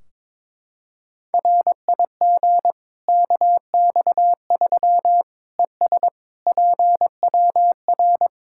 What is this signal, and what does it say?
Morse code, heard as a keyed tone, about 21 wpm, 700 Hz. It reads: RIG KX3 ES PWR
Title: Practice QSO - Exchange 2 - Info - 22wpm